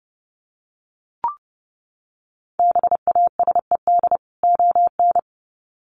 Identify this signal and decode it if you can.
Morse code, heard as a keyed tone, about 30 wpm, 700 Hz. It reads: BASED ON
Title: Sets of 2 Words 30wpm